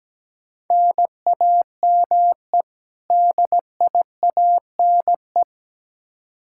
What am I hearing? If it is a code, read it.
Morse code, heard as a keyed tone, about 17 wpm, 700 Hz. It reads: NAME DIANE
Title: K1USN SST QSO Practice 17wpm